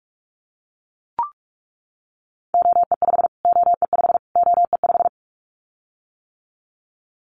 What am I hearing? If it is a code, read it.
Morse code, heard as a keyed tone, about 44 wpm, 700 Hz. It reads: OE5 OE5 OE5
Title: Callsign Prefixes for Top 25 DXCC - Repeated 3x - 45wpm